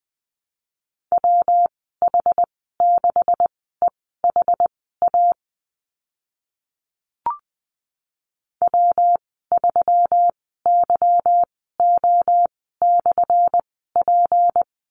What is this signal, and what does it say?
Morse code, heard as a keyed tone, about 20 wpm, spaced as for 15 wpm, 700 Hz. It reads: WH6EHA W3YO/P
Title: 179 Lesson 29g - Slash - Call Signs 20wpm